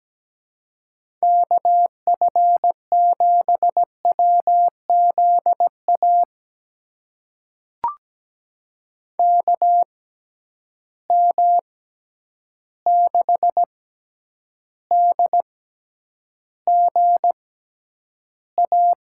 Morse code, heard as a keyed tone, about 17 wpm, spaced as for 9 wpm, 700 Hz. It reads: KF7WZA KM6DGA
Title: USA Call Signs - 6 Characters - 6x Character Spacing - 17wpm